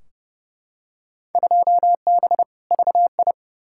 Morse code, heard as a keyed tone, about 30 wpm, 700 Hz. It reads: 2B VI
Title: Field Day Exchanges Encore - 30wpm